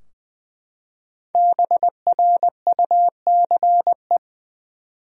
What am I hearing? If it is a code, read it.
Morse code, heard as a keyed tone, about 20 wpm, 700 Hz. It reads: BRUCE
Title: USA Common Names - Max 5 Letters - 20wpm